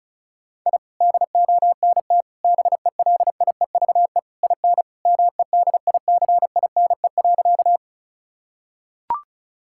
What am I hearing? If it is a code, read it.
Morse code, heard as a keyed tone, about 36 wpm, 700 Hz. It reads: I DONT BELIEVE IN MEDICINE.
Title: Sentences from Top 1000 Words - Review Part 6 - 35wpm